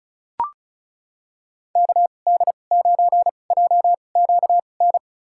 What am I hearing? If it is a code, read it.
Morse code, heard as a keyed tone, about 36 wpm, spaced as for 25 wpm, 700 Hz. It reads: KD9JQN
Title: USA Call Signs - 6 Characters - 2x Character Spacing - 35wpm